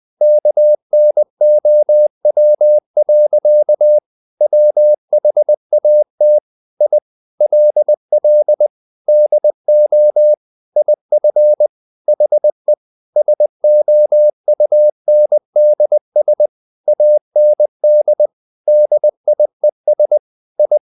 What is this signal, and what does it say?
Morse code, heard as a keyed tone, about 20 wpm, 600 Hz. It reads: KNOW. WHAT I LL DO IF HE SOUNDS AND DIES I
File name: the old man and the sea / the old man and the sea part 1 20wpm